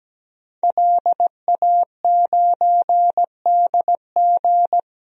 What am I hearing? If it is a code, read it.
Morse code, heard as a keyed tone, about 17 wpm, 700 Hz. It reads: LA9DG
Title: DXCC-100 Call Signs - 5 Characters - 17wpm